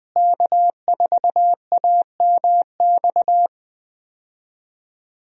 Morse code, heard as a keyed tone, about 20 wpm, 700 Hz. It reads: K4AMX
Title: USA Call Signs - 5 Characters - Repeated 2x - 20wpm